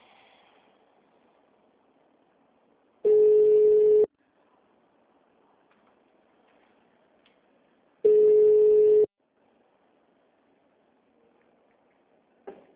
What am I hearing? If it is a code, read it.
Ringback tone (Europe, 425 Hz).
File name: out- -2004- - - .140